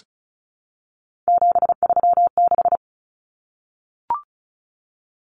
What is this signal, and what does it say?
Morse code, heard as a keyed tone, about 34 wpm, 700 Hz. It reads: 736
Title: Common Cut Numbers - 3-Digit 35wpm